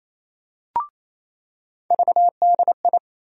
Morse code, heard as a keyed tone, about 28 wpm, 700 Hz. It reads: VDI
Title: USA Call Sign Postfixes - 3 Letters 28wpm